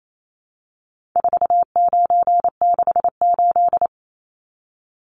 Morse code, heard as a keyed tone, about 28 wpm, 700 Hz. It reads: 4968
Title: Numbers - 4-Digit 28wpm